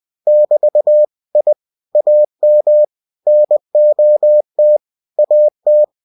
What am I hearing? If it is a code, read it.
Morse code, heard as a keyed tone, about 20 wpm, 600 Hz. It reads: <BT> I AM NOT AT
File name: cw-books / winnie-the-pooh / winnie-20wpm-page-0071